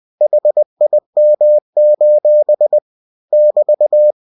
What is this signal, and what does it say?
Morse code, heard as a keyed tone, about 20 wpm, 600 Hz. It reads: HIM: <BT>
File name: cw-books / winnie-the-pooh / winnie-20wpm-page-0045